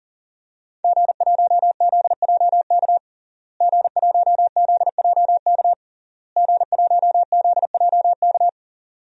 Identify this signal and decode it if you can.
Morse code, heard as a keyed tone, about 37 wpm, 700 Hz. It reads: G1ZJK G1ZJK G1ZJK
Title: DXCC-100 Call Signs - 5 Characters - Repeated 3x - 40wpm